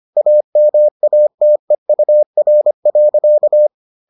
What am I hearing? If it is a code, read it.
Morse code, heard as a keyed tone, about 25 wpm, 600 Hz. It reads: AMATEUR.